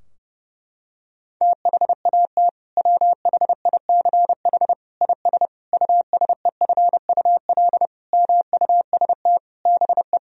Morse code, heard as a keyed tone, about 30 wpm, 700 Hz. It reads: THAT WHICH IS USEFUL MUST BE
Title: Sentences from Top 1000 Words - Review Part 4 - 30wpm